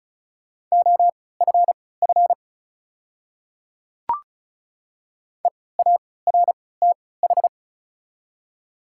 Morse code, heard as a keyed tone, about 36 wpm, spaced as for 20 wpm, 690 Hz. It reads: OFF EARTH
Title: 3x Character Spacing - Top 500 Words - 35wpm